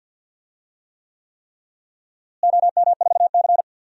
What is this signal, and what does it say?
Morse code, heard as a keyed tone, about 47 wpm, 700 Hz. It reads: OM4C
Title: DXCC-100 Call Signs - 4 Characters - Repeated 3x - 50wpm